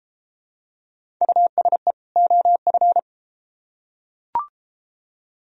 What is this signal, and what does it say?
Morse code, heard as a keyed tone, about 32 wpm, 700 Hz. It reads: USE OF